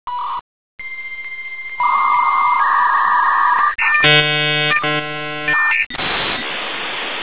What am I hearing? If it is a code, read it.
Dial-up modem handshake.